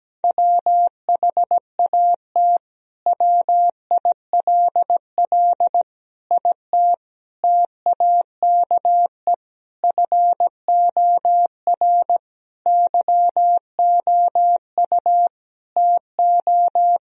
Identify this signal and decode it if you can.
Morse code, heard as a keyed tone, about 17 wpm, 700 Hz. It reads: WHAT WILL IT TAKE FOR YOU TO
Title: Sentences from Top 1000 Words - Review Part 3 - 17wpm